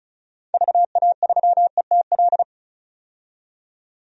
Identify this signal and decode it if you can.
Morse code, heard as a keyed tone, about 36 wpm, 700 Hz. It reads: VA3ETL
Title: DXCC-100 Call Signs - 6 Characters - 35wpm